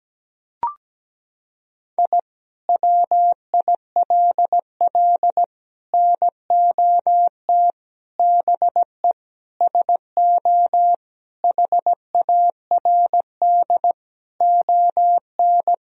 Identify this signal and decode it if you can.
Morse code, heard as a keyed tone, about 17 wpm, 700 Hz. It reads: I WILL NOT BE SO HARD ON
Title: Sentences from Top 1000 Words - Review Part 1 - 17wpm